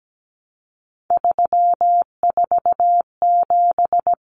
Morse code, heard as a keyed tone, about 17 wpm, 700 Hz. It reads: 347
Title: Numbers - 3-Digit 17wpm